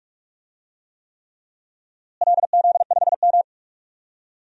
Morse code, heard as a keyed tone, about 46 wpm, 700 Hz. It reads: LZ5M
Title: DXCC-100 Call Signs - 4 Characters - Repeated 3x - 45wpm